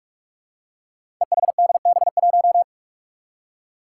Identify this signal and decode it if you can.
Morse code, heard as a keyed tone, about 46 wpm, 700 Hz. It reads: EHDB1
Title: Five Challenging Letter-Number Combinations 45wpm